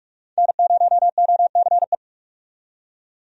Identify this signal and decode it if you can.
Morse code, heard as a keyed tone, about 45 wpm, 700 Hz. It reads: N0OCE
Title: USA Call Signs - 5 Characters - 45wpm